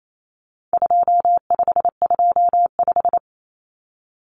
Morse code, heard as a keyed tone, about 28 wpm, 700 Hz. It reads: 2525
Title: Common Cut Numbers - 4-Digit 28wpm